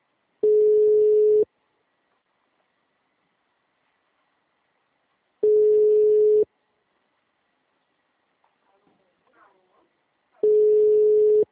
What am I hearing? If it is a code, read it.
Ringback tone (Europe, 425 Hz), with speech.